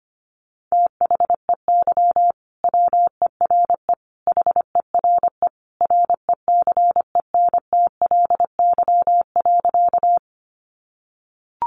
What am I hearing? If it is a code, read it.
Morse code, heard as a keyed tone, about 25 wpm, 700 Hz. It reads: THEY WERE HERE RECENTLY.
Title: Sentences from Top 1200 Words 25wpm